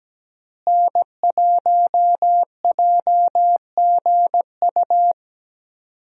Morse code, heard as a keyed tone, about 17 wpm, 700 Hz. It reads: N1JGU
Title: USA Call Signs - 5 Characters - Repeated 2x - 17wpm